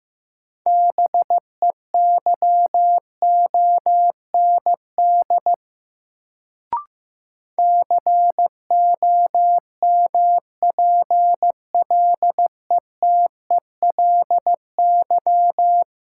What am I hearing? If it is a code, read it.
Morse code, heard as a keyed tone, about 15 wpm, 700 Hz. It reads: BEYOND COMPLETELY
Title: Sentences from Top 1100 Words - New Word Drill Alt - 15wpm